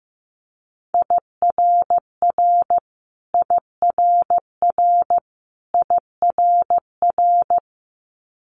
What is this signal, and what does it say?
Morse code, heard as a keyed tone, about 15 wpm, 700 Hz. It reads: IRR IRR IRR
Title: Multi-letter Phonemes Repeated 3x 15wpm